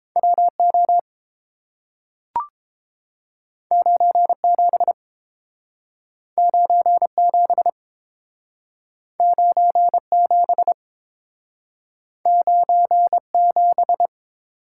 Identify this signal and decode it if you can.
Morse code, heard as a keyed tone, about 30 wpm, 700 Hz. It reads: WO 97 97 97 97